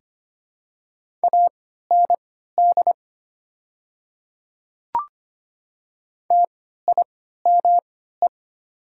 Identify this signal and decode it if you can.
Morse code, heard as a keyed tone, about 25 wpm, spaced as for 14 wpm, 690 Hz. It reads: AND TIME
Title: 3x Character Spacing - Top 200 Words - 25wpm